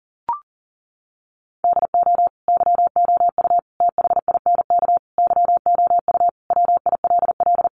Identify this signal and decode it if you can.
Morse code, heard as a keyed tone, about 40 wpm, 700 Hz. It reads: DO YOU THINK YOU WILL
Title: Sentences from Top 1300 Words 40wpm